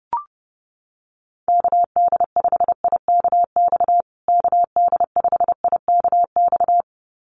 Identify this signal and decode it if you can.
Morse code, heard as a keyed tone, about 30 wpm, 700 Hz. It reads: KD5IKX KD5IKX
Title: USA Call Signs Repeated 2x 30wpm